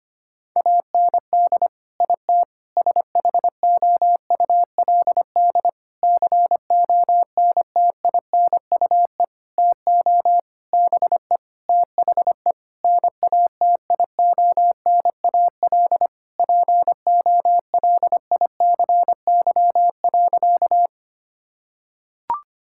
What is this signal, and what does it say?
Morse code, heard as a keyed tone, about 25 wpm, 700 Hz. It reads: AND IT SHOULD CONTINUE TO BE THE NATIONAL POLICY.
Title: Sentences from Top 1000 Words - Review Part 6 - 25wpm